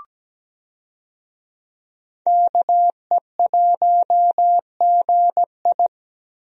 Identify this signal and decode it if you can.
Morse code, heard as a keyed tone, about 17 wpm, 700 Hz. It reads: KE1GI
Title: USA Call Signs - 5 Characters - Repeated 2x - 17wpm